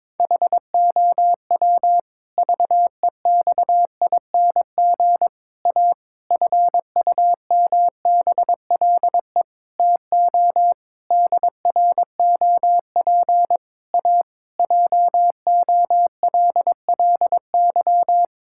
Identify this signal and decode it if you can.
Morse code, heard as a keyed tone, about 21 wpm, 700 Hz. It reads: HOW VEXING A FUMBLE TO DROP A JOLLY
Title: Pangram Challenge v2 - 22wpm